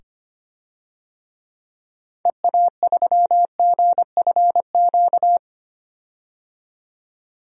Morse code, heard as a keyed tone, about 25 wpm, 700 Hz. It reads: EA3GFQ
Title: DXCC-100 Call Signs - 6 Characters - 25wpm